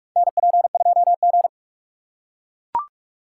Morse code, heard as a keyed tone, about 45 wpm, 700 Hz. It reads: NP2G
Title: USA Call Signs - 4 Characters - 6x Character Spacing - 45wpm